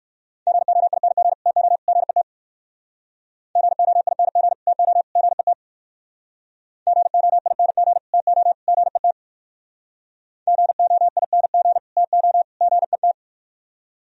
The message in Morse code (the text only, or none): GOING TO GET GOING TO GET GOING TO GET GOING TO GET